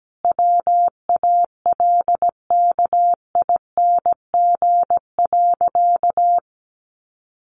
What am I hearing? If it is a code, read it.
Morse code, heard as a keyed tone, about 17 wpm, 700 Hz. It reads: WALKING.
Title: Easy Phrases 17wpm